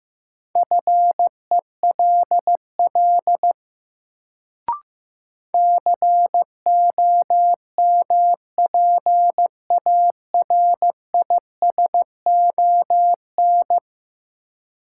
Morse code, heard as a keyed tone, about 16 wpm, 700 Hz. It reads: FELL COMPARISON
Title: Sentences from Top 1600 Words - New Word Drill - 15wpm